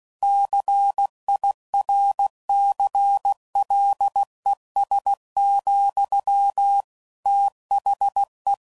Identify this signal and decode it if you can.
Morse code, heard as a keyed tone, about 16 wpm, 800 Hz. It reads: CIRCLES, THE
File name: CW Code Groups @15WPM / Technical Subjects / XC